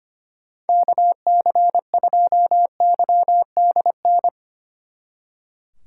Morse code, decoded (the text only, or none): KC2YDN